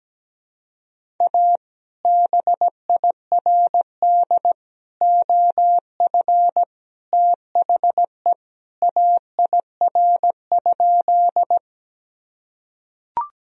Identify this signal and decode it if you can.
Morse code, heard as a keyed tone, about 17 wpm, 700 Hz. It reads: A BIRD OF THE AIR?
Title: Sentences from Top 1000 Words - Review Part 2 - 17wpm